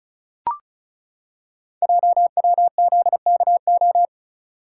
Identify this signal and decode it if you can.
Morse code, heard as a keyed tone, about 36 wpm, 700 Hz. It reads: JWZKO
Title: Code Groups - 5 Letters - 35wpm